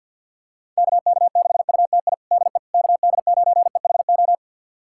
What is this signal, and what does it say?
Morse code, heard as a keyed tone, about 47 wpm, 700 Hz. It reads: KK6UTI DE KD9EHO
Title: USA Callsign DE Callsign 50wpm